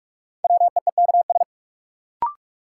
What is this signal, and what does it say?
Morse code, heard as a keyed tone, about 46 wpm, 700 Hz. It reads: WEEKS
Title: Sets of 5 Words - New Word Drill - 45wpm